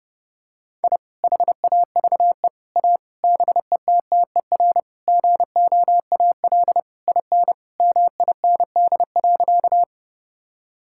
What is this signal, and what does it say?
Morse code, heard as a keyed tone, about 30 wpm, 700 Hz. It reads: I HAVE A BETTER GOAL IN MIND.
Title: Sentences from Top 1000 Words - Review Part 5 - 30wpm